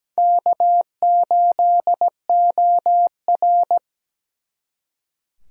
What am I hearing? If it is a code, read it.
Morse code, heard as a keyed tone, about 20 wpm, 700 Hz. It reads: K8OR
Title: Speed Racing - USA 1x2 Call Signs 17wpm